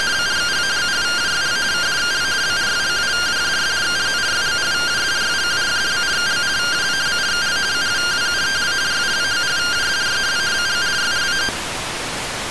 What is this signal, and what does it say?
Two-tone teleprinter signal, about 170 Hz shift.
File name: findings / MR AXIAL 14.gen-02 rtty rx